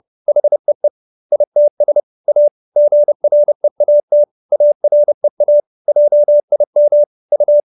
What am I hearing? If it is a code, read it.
Morse code, heard as a keyed tone, about 30 wpm, 590 Hz. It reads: HEE ITS A GREAT AREA JIM U